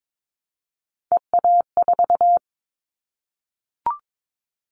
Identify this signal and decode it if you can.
Morse code, heard as a keyed tone, about 22 wpm, 700 Hz. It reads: EA4
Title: Callsign Prefixes for Top 25 DXCC - 22wpm